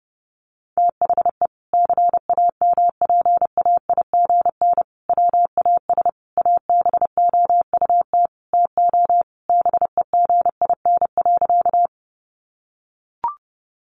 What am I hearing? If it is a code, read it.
Morse code, heard as a keyed tone, about 30 wpm, 700 Hz. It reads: THE CAMPAIGN WAS ABOUT TO BEGIN.
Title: Sentences from Top 1200 Words 30wpm